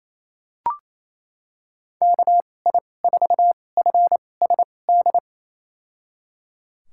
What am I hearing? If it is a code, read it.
Morse code, heard as a keyed tone, about 28 wpm, spaced as for 20 wpm, 700 Hz. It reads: KI4FSD
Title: USA Call Signs - 6 Characters - 2x Character Spacing - 28wpm